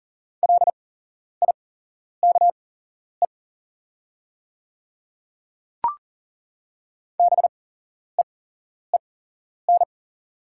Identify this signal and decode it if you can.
Morse code, heard as a keyed tone, about 40 wpm, spaced as for 11 wpm, 700 Hz. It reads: LIKE BEEN
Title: 8x Character Spacing - Top 100 Words - 40wpm